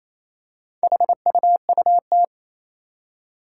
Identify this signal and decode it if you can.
Morse code, heard as a keyed tone, about 28 wpm, 700 Hz. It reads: HUUT
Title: Four Challenging Letter-Number Combinations 28wpm